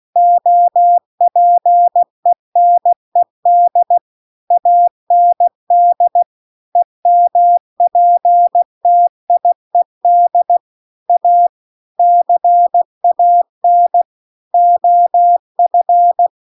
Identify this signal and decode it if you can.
Morse code, heard as a keyed tone, about 16 wpm, 700 Hz. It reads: OPENED AND EMPTIED A CAN OF